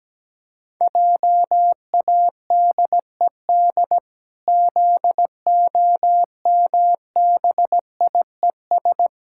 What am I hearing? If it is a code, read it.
Morse code, heard as a keyed tone, about 17 wpm, 700 Hz. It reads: JADED ZOMBIES
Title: Pangram Challenge v2 - 17wpm